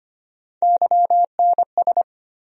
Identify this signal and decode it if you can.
Morse code, heard as a keyed tone, about 25 wpm, 700 Hz. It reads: YNS
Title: USA Call Sign Postfixes - 3 Letters 25wpm